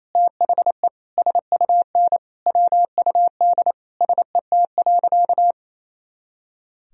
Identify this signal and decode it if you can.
Morse code, heard as a keyed tone, about 27 wpm, 700 Hz. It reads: THE SUN WUD SET.
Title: Sentences from Top 400 Words and Abbreviations 28wpm